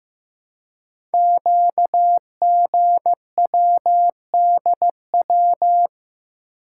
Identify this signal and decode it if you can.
Morse code, heard as a keyed tone, about 15 wpm, 700 Hz. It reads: QGWDW
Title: Code Groups - 5 Letters - 15wpm